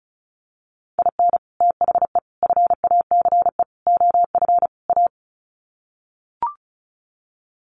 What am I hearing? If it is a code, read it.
Morse code, heard as a keyed tone, about 34 wpm, 700 Hz. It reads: IN THE FACE OF A
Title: Sets of 5 Words Encore 35wpm